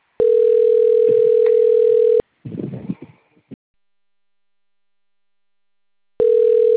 Ringback tone (North America), 440 + 480 Hz.